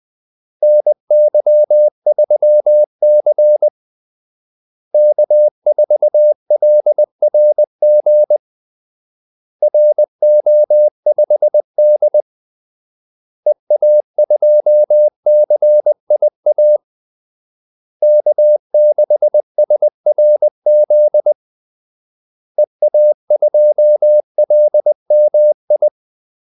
Morse code, heard as a keyed tone, about 20 wpm, 600 Hz. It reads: NY3C K4LRG RO5D EA2CIA K6SRZ EA2LMI
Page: Kaldesignaler 20 wpm